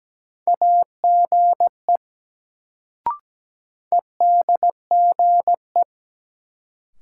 Morse code, heard as a keyed tone, about 17 wpm, 700 Hz. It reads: AGE EDGE
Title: Sets of 3 Words - New Word Drill - 17wpm